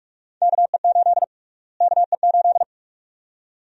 Morse code, heard as a keyed tone, about 45 wpm, 700 Hz. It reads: KE8 KE8
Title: USA Call Sign Prefixes - Repeated 2x - 45wpm